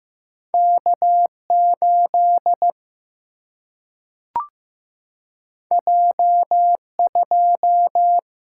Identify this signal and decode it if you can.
Morse code, heard as a keyed tone, about 15 wpm, 700 Hz. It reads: K 8 J 2
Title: Single-Letter Single-Number 15wpm